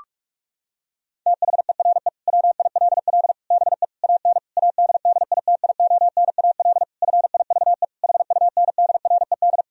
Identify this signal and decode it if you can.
Morse code, heard as a keyed tone, about 46 wpm, 700 Hz. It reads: THERE WILL BE AN ADDITIONAL FIVE HUNDRED
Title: Sentences from Top 1000 Words - Review Part 4 - 45wpm